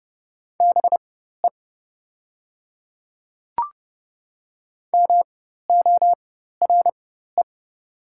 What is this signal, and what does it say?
Morse code, heard as a keyed tone, about 30 wpm, spaced as for 14 wpm, 700 Hz. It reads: BE MORE
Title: 4x Character Spacing - Top 100 Words - 30wpm